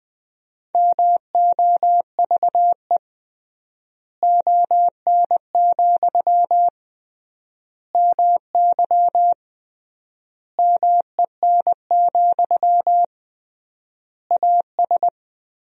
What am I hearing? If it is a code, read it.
Morse code, heard as a keyed tone, about 20 wpm, 700 Hz. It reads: MOVE ON, MY MEN, AS
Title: 3x Word-Spacing - Sentences from Top 500 Words - 20wpm